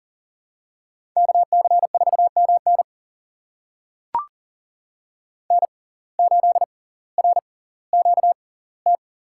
Morse code, heard as a keyed tone, about 40 wpm, spaced as for 20 wpm, 700 Hz. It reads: KC4MN N8RQT
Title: USA Call Signs - 5 Characters - 6x Character Spacing - 40wpm